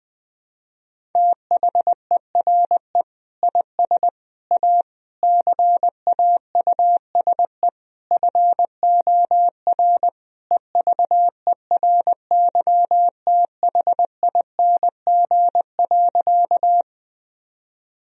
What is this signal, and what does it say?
Morse code, heard as a keyed tone, about 21 wpm, 700 Hz. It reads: THERE IS A CAUSE FOR EVERYTHING.
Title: Sentences from Top 1000 Words - Review Part 1 - 20wpm